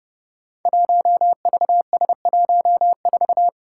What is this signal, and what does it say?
Morse code, heard as a keyed tone, about 30 wpm, 700 Hz. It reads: 1VS14
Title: Five Challenging Letter-Number Combinations 30wpm